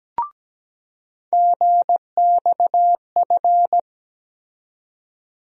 Morse code, heard as a keyed tone, about 17 wpm, 700 Hz. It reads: GXF
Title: USA Call Sign Postfixes - 3 Letters 17wpm